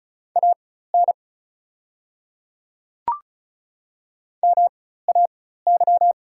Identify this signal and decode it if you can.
Morse code, heard as a keyed tone, about 36 wpm, spaced as for 16 wpm, 700 Hz. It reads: AN MAY
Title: 4x Character Spacing - Top 100 Words - 35wpm